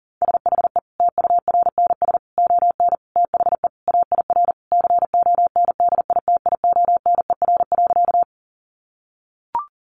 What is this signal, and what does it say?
Morse code, heard as a keyed tone, about 40 wpm, 700 Hz. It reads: SHE TURNS ON THE AIR CONDITIONER.
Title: Additional Easy Phrases 40wpm